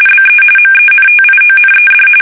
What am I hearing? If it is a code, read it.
RTTY, about 45.45 baud. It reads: BE MENTIONED